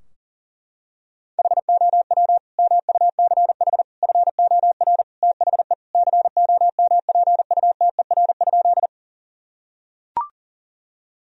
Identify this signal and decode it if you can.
Morse code, heard as a keyed tone, about 40 wpm, 700 Hz. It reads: HOW MUCH FOR THE COMPUTER?
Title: Sentences from Top 1000 Words - Review Part 3 - 40wpm